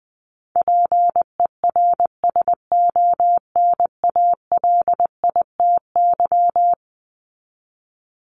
Morse code, heard as a keyed tone, about 20 wpm, 700 Hz. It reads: PERSONALITY
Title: Top 1251-1500 Words 20wpm